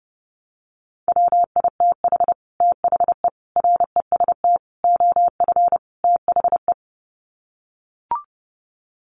Morse code, heard as a keyed tone, about 30 wpm, 700 Hz. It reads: WITH THE REST OF THE
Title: Sets of 5 Words 30wpm